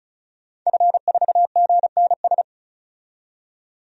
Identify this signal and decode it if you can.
Morse code, heard as a keyed tone, about 36 wpm, 700 Hz. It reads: F4GNS
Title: DXCC-100 Call Signs - 5 Characters - 35wpm